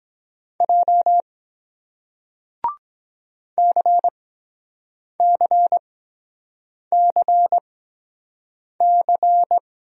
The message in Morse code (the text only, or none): J CCCC